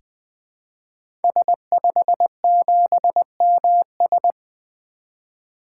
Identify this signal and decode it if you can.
Morse code, heard as a keyed tone, about 20 wpm, 700 Hz. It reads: S57MS
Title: 2023 SOTA Activator Call Signs - 20wpm